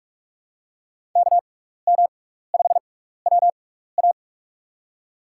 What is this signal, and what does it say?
Morse code, heard as a keyed tone, about 45 wpm, spaced as for 15 wpm, 700 Hz. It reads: KM5WA
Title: USA Call Signs - 5 Characters - 6x Character Spacing - 45wpm